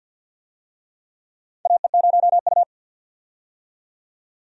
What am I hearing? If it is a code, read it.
Morse code, heard as a keyed tone, about 52 wpm, 700 Hz. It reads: AE0U
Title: USA Call Signs - 4 Characters - 50wpm